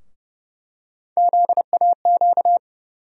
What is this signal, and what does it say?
Morse code, heard as a keyed tone, about 30 wpm, 700 Hz. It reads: ZAQ
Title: USA Call Sign Postfixes - 3 Letters 30wpm